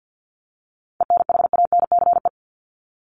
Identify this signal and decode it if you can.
Morse code, heard as a keyed tone, about 50 wpm, 700 Hz. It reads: ENHANCE
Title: Top 1251-1500 Words 50wpm